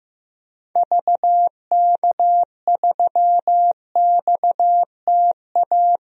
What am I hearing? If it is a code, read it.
Morse code, heard as a keyed tone, about 15 wpm, 700 Hz. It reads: VK3XTA
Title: DXCC-100 Call Signs - 6 Characters - 15wpm